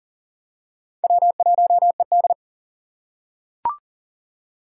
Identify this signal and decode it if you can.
Morse code, heard as a keyed tone, about 40 wpm, 690 Hz. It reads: W1ED
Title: USA Call Signs - 4 Characters - 2x Character Spacing - 40wpm